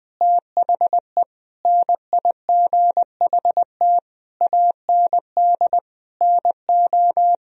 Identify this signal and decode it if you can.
Morse code, heard as a keyed tone, about 19 wpm, 700 Hz. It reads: THE NIGHT AND NO
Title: Sentences from Top 1000 Words - Review Part 3 - 20wpm